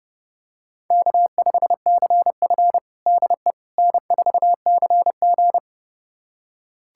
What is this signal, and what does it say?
Morse code, heard as a keyed tone, about 30 wpm, 700 Hz. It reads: K5CF DE N4CG
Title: USA Callsign DE Callsign - 4 Characters - 30wpm